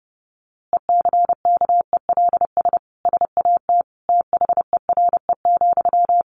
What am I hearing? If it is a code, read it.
Morse code, heard as a keyed tone, about 30 wpm, 700 Hz. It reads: ECKELS SAT THERE,
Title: A Sound Of Thunder 30wpm